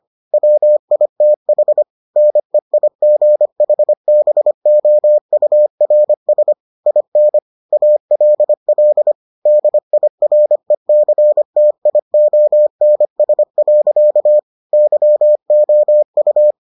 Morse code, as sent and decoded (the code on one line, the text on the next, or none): .-- .. - .... / -. .. --. .... -... --- ..- .-. ... / .. -. / .- .-.. .-.. / -.. .. .-. -.-. - .. --- -. ... .-.-.- / -.-- --- ..-
WITH NEIGHBOURS IN ALL DIRECTIONS. YOU